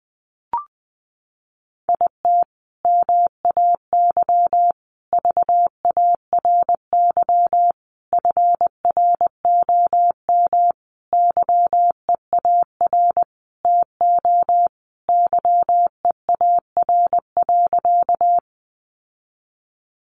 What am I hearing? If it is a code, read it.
Morse code, heard as a keyed tone, about 20 wpm, 700 Hz. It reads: IT MAY VARY FROM YEAR TO YEAR.
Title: Sentences from Top 1200 Words 20wpm